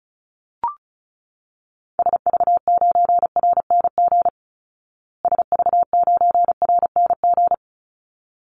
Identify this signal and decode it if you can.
Morse code, heard as a keyed tone, about 34 wpm, 700 Hz. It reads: SV9RNG SV9RNG
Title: Call Signs Repeated 3x - 35wpm